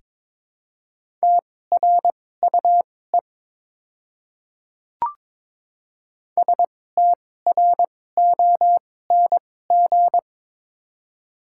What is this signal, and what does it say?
Morse code, heard as a keyed tone, about 22 wpm, spaced as for 16 wpm, 700 Hz. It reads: TRUE STRONG
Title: 2x Character Spacing - Top 500 Words - 22wpm